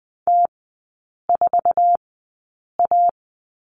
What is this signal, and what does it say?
Morse code, heard as a keyed tone, about 20 wpm, spaced as for 8 wpm, 700 Hz. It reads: T4A
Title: 074 Lesson 13e - Letter L - Three Characters 20wpm